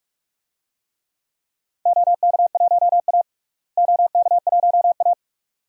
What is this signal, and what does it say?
Morse code, heard as a keyed tone, about 46 wpm, 700 Hz. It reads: OK1A OK1A
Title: DXCC-100 Call Signs - 4 Characters - Repeated 2x - 45wpm